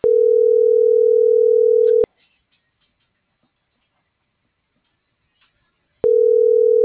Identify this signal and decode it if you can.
Ringback tone (North America), 440 + 480 Hz.